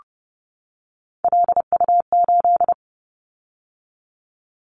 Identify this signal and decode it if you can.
Morse code, heard as a keyed tone, about 30 wpm, 700 Hz. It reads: LU8
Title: Callsign Prefixes for Top 100 DXCC - 30wpm